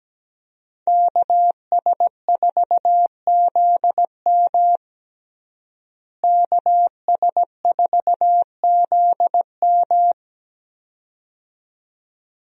Morse code, heard as a keyed tone, about 17 wpm, 700 Hz. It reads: KS4ZM KS4ZM
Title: USA Call Signs - 5 Characters - Repeated 2x - 17wpm